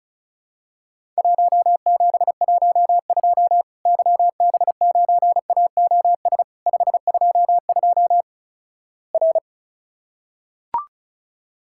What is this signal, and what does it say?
Morse code, heard as a keyed tone, about 36 wpm, 700 Hz. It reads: 1712 YB9AOS 522 R
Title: Worked All Europe DX Contest - QTC Exchange - 35wpm